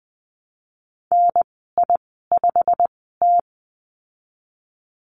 Morse code, heard as a keyed tone, about 19 wpm, spaced as for 14 wpm, 700 Hz. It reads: NI5T
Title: 088 Lesson 15g - Number 5 - Call Signs 20wpm